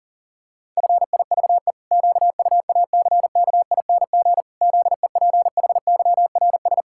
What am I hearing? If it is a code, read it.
Morse code, heard as a keyed tone, about 40 wpm, 700 Hz. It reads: FIVE QUACKING ZEPHYRS
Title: Pangram Challenge v2 - 40wpm